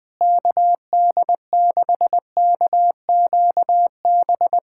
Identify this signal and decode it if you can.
Morse code, heard as a keyed tone, about 21 wpm, 700 Hz. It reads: KD6KQB
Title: USA Callsign DE Callsign 20wpm